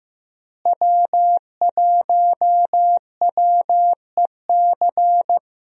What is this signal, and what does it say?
Morse code, heard as a keyed tone, about 15 wpm, 700 Hz. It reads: W1WEC
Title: USA Call Signs - 5 Characters - Repeated 3x - 15wpm